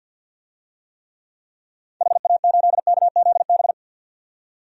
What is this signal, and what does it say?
Morse code, heard as a keyed tone, about 52 wpm, 700 Hz. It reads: HA8KZB
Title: DXCC-100 Call Signs - 6 Characters - 50wpm